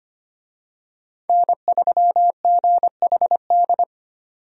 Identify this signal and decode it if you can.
Morse code, heard as a keyed tone, about 25 wpm, 700 Hz. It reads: N3GHD
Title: USA Call Signs - 5 Characters - 25wpm